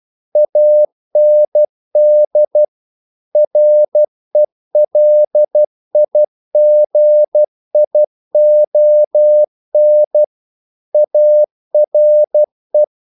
Morse code, heard as a keyed tone, about 12 wpm, 600 Hz. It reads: AND RELIGION ARE
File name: CW / Practice / EnglishQuotations0112